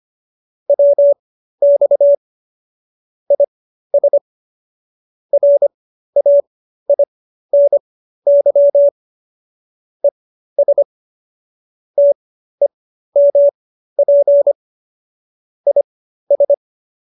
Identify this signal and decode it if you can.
Morse code, heard as a keyed tone, about 25 wpm, spaced as for 13 wpm, 570 Hz. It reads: WX IS RAINY ES TEMP IS